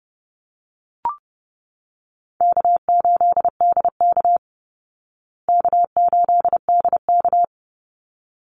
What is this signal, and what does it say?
Morse code, heard as a keyed tone, about 28 wpm, 700 Hz. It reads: K8DK K8DK
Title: Easy Call Signs Repeated 2x - 30wpm